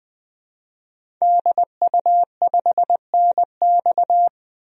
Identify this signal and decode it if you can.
Morse code, heard as a keyed tone, about 20 wpm, 700 Hz. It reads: DU5NX
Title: DXCC-100 Call Signs - 5 Characters - 20wpm